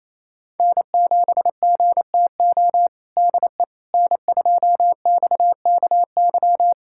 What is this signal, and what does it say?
Morse code, heard as a keyed tone, about 28 wpm, 700 Hz. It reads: N7GTO DE N2XKY
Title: USA Callsign DE Callsign - 5 Characters - 28wpm